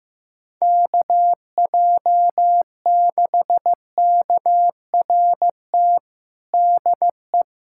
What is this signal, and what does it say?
Morse code, heard as a keyed tone, about 15 wpm, 700 Hz. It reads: KJ6KRT DE
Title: USA Callsign DE Callsign 15wpm